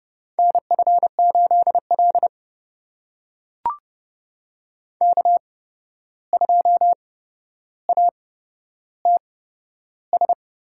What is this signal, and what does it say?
Morse code, heard as a keyed tone, about 30 wpm, spaced as for 12 wpm, 700 Hz. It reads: NF8L K2ATS
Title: USA Call Signs - 8x Character Spacing - 30wpm